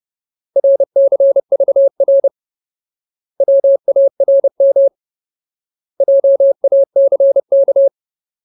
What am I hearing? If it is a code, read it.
Morse code, heard as a keyed tone, about 30 wpm, 550 Hz. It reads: RCVR WARM JACK
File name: Ultimate Word List / 30WPM Ultimate